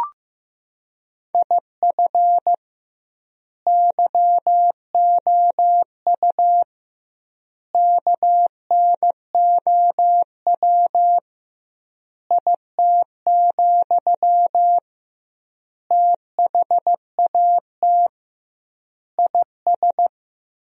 Morse code, heard as a keyed tone, about 15 wpm, 700 Hz. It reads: IF YOU KNOW IT, THAT IS
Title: 2x Word-Spacing - Sentences from Top 400 Words - 15wpm